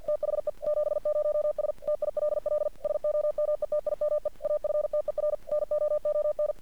Morse code, heard as a keyed tone, about 50 wpm, 600 Hz. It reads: THE 70S TILL SOMETIME AFTER NOON
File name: archived / CRSnet